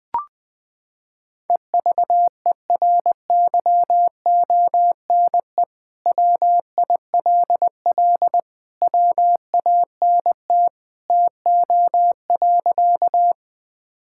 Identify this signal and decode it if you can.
Morse code, heard as a keyed tone, about 20 wpm, 700 Hz. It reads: EVERYONE WILL WANT TO.
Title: Sentences from Top 1000 Words - Review Part 5 - 20wpm